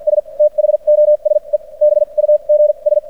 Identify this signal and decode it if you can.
Morse code, heard as a keyed tone, about 53 wpm, spaced as for 39 wpm, 600 Hz. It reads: S THOSE BUGS
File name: archived / CRSnet